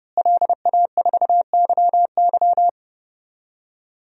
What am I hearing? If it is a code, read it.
Morse code, heard as a keyed tone, about 32 wpm, 700 Hz. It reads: LA4YY
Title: DXCC-100 Call Signs - 5 Characters - Repeated 2x - 30wpm